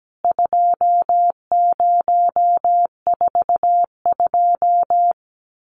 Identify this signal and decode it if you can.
Morse code, heard as a keyed tone, about 17 wpm, 700 Hz. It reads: 2042
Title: CWT Exchange 17wpm